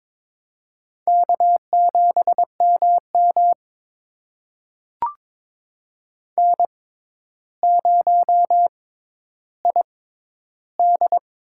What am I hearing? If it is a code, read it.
Morse code, heard as a keyed tone, about 22 wpm, spaced as for 11 wpm, 700 Hz. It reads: K7MM N0ID
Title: USA Call Signs - 4 Characters - 6x Character Spacing - 22wpm